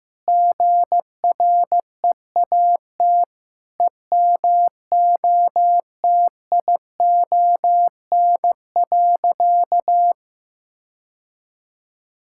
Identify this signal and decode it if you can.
Morse code, heard as a keyed tone, about 15 wpm, 700 Hz. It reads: GREAT EMOTION.
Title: Sentences from Top 1600 Words 15wpm